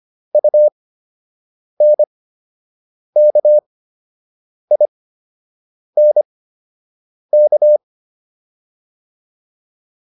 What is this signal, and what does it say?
Morse code, heard as a keyed tone, about 25 wpm, spaced as for 7 wpm, 600 Hz. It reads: UNKINK